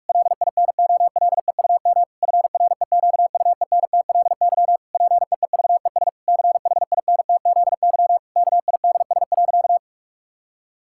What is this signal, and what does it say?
Morse code, heard as a keyed tone, about 46 wpm, 700 Hz. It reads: LINOLEUM FREQUENTLY PEEVES CHINTZY KIDS.
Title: Pangram Challenge v2 - 45wpm